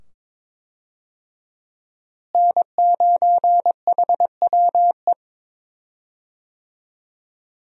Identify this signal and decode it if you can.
Morse code, heard as a keyed tone, about 21 wpm, 700 Hz. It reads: N9HWE
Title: USA Call Signs - 5 Characters - Repeated 2x - 22wpm